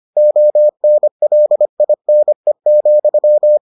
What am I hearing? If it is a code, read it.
Morse code, heard as a keyed tone, about 25 wpm, 600 Hz. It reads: ONLINE,